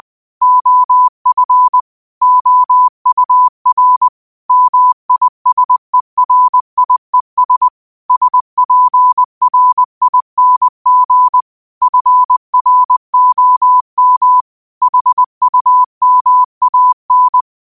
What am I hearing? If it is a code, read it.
Morse code, heard as a keyed tone, about 20 wpm, 1000 Hz. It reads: OF OUR MISERIES SPRING FROM HUMAN